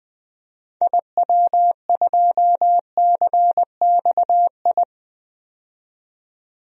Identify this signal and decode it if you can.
Morse code, heard as a keyed tone, about 20 wpm, 700 Hz. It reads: IW2CXI
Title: DXCC-100 Call Signs - 6 Characters - 20wpm